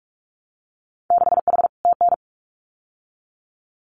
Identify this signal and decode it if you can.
Morse code, heard as a keyed tone, about 46 wpm, 700 Hz. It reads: 6H TN
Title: Winter Field Day Exchanges Encore 45wpm